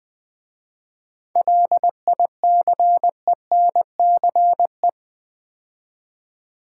Morse code, heard as a keyed tone, about 20 wpm, 700 Hz. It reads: LICENCE
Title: 7 Letter Words - First 3 Letters Given - 20wpm